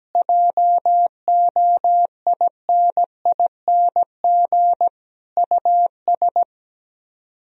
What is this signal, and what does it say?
Morse code, heard as a keyed tone, about 17 wpm, 700 Hz. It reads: JOINING US
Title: Speed-Racing - Sets of 5 Words 17wpm